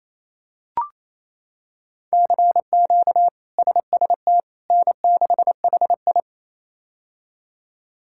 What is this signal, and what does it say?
Morse code, heard as a keyed tone, about 28 wpm, 700 Hz. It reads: CQ SST N6HI
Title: K1USN SST QSO Practice - 28wpm